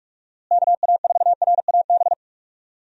Morse code, heard as a keyed tone, about 45 wpm, 700 Hz. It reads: KA4RAB
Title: USA Call Signs - 6 Characters - Repeated 2x - 45wpm